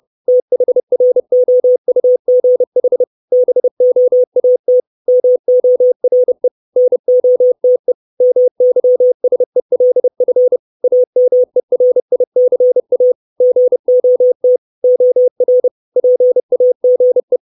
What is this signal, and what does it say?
Morse code, heard as a keyed tone, about 30 wpm, 500 Hz. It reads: THROUGH BOAT MORE NOTE MYSELF AMERICA GOT OR PAGE
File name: 632 Words 30WPM